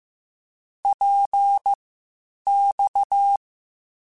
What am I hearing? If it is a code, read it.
Morse code, heard as a keyed tone, about 15 wpm, spaced as for 8 wpm, 800 Hz. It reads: PX
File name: Short Runs / Short Runs-mp3s / Balanced @13